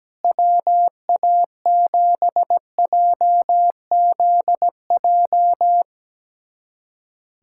Morse code, heard as a keyed tone, about 17 wpm, 700 Hz. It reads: WA7JZJ
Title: USA Callsign DE Callsign - 6 Characters - 17wpm